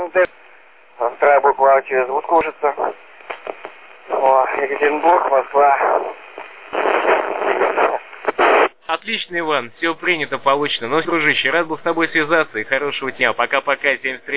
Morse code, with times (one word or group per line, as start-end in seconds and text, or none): none